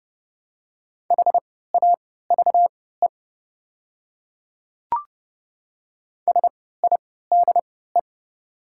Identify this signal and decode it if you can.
Morse code, heard as a keyed tone, about 31 wpm, spaced as for 17 wpm, 700 Hz. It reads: HAVE SIDE
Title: 3x Character Spacing - Top 100 Words - 30wpm